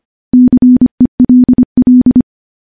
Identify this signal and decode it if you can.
Morse code, heard as a keyed tone, about 25 wpm, 250 Hz. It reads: CELL
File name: morse / morse-training-25WPM